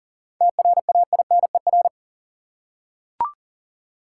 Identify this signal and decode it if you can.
Morse code, heard as a keyed tone, about 40 wpm, 700 Hz. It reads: TRAINER
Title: Seven Letter Words - 40wpm